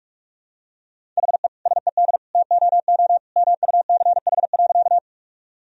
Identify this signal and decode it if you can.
Morse code, heard as a keyed tone, about 41 wpm, 700 Hz. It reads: HE SED TOO MUCH.
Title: Sentences from Top 200 Words and Abbreviations 45wpm